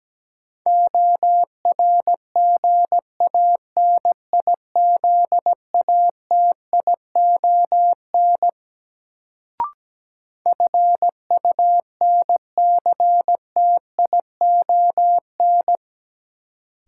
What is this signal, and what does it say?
Morse code, heard as a keyed tone, about 17 wpm, 700 Hz. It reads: ORGANIZATION FUNCTION
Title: Sentences from Top 1100 Words - New Word Drill Alt - 17wpm